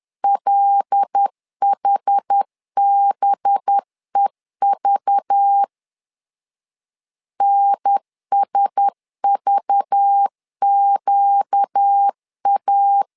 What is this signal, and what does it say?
Morse code, heard as a keyed tone, about 10 wpm, 800 Hz. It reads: LHBEV NSVQA
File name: LEZIONI / 5 LBVC / 5 LBV 10wpm 2248